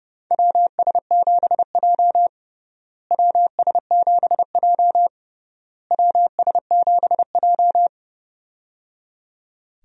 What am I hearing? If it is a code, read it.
Morse code, heard as a keyed tone, about 30 wpm, 700 Hz. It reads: WS7J WS7J WS7J
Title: USA Call Signs - 4 Characters - Repeated 3x - 30wpm